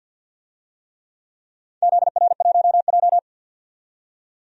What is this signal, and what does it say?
Morse code, heard as a keyed tone, about 52 wpm, 700 Hz. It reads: ZR1J
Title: DXCC-100 Call Signs - 4 Characters - 50wpm